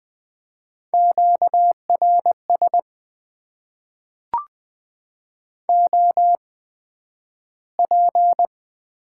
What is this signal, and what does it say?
Morse code, heard as a keyed tone, about 20 wpm, spaced as for 11 wpm, 700 Hz. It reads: QRS OP
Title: Top 90 QSO Elements - 8x Character Spacing - 20wpm